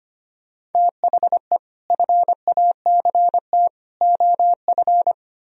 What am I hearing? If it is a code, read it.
Morse code, heard as a keyed tone, about 33 wpm, 700 Hz. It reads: THE FACT OF